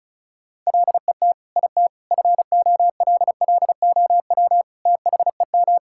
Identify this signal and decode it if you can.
Morse code, heard as a keyed tone, about 36 wpm, 700 Hz. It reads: LET IT FOLLOW THEM
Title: Sentences from Top 1600 Words 35wpm